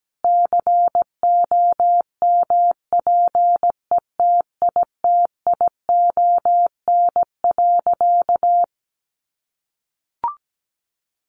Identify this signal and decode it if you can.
Morse code, heard as a keyed tone, about 17 wpm, 700 Hz. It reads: COMPETITION.
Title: Sentences from Top 1000 Words 17wpm